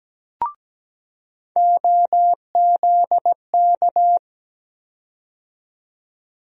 Morse code, heard as a keyed tone, about 17 wpm, 700 Hz. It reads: OZK
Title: USA Call Sign Postfixes - 3 Letters 17wpm